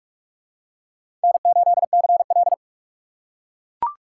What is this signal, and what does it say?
Morse code, heard as a keyed tone, about 40 wpm, 700 Hz. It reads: N8CL